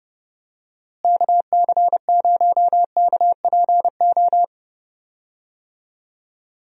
Morse code, heard as a keyed tone, about 32 wpm, 700 Hz. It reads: KC0KPO
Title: USA Call Signs - 6 Characters - Repeated 3x - 30wpm